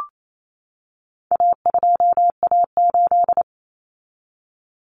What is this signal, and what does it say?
Morse code, heard as a keyed tone, about 28 wpm, 700 Hz. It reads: A2A8
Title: Common Cut Numbers - 4-Digit 28wpm